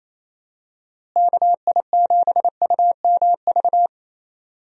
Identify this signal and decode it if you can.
Morse code, heard as a keyed tone, about 28 wpm, 700 Hz. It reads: KI7UMV
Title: USA Call Signs - 6 Characters - Repeated 3x - 28wpm